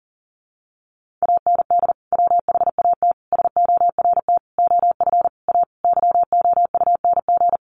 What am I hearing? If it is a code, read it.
Morse code, heard as a keyed tone, about 40 wpm, 700 Hz. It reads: AND WHAT SORT OF A YOUNG
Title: Sentences from Top 1300 Words 40wpm